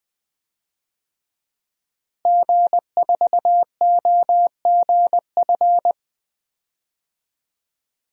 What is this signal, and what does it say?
Morse code, heard as a keyed tone, about 20 wpm, 700 Hz. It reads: G4OGF
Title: DXCC-100 Call Signs - 5 Characters - Repeated 2x - 20wpm